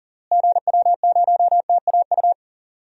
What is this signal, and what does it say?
Morse code, heard as a keyed tone, about 37 wpm, 700 Hz. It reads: GW0TAU
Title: DXCC-100 Call Signs - 6 Characters - Repeated 3x - 40wpm